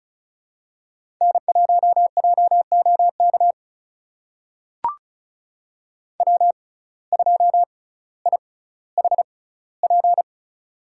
Morse code, heard as a keyed tone, about 34 wpm, spaced as for 18 wpm, 700 Hz. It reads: N1JOK W2IHP
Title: USA Call Signs - 5 Characters - 6x Character Spacing - 35wpm